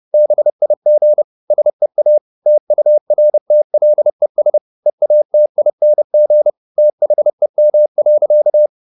Morse code, heard as a keyed tone, about 30 wpm, 600 Hz. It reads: BIG SEA TURTLES EATING THEM.
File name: the old man and the sea / the old man and the sea 1 30wpm